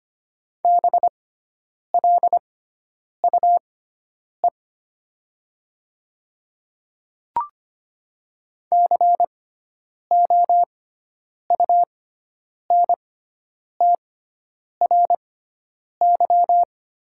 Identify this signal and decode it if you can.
Morse code, heard as a keyed tone, about 25 wpm, spaced as for 9 wpm, 700 Hz. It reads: BLUE COUNTRY
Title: 6x Character Spacing - Top 500 Words - 25wpm